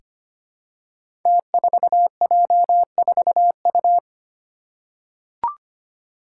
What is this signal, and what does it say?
Morse code, heard as a keyed tone, about 25 wpm, 700 Hz. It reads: T4J4U
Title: Five Challenging Letter-Number Combinations 25wpm